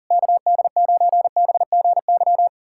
Morse code, heard as a keyed tone, about 40 wpm, 700 Hz. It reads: KD9BGY
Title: USA Callsign DE Callsign - 6 Characters - 40wpm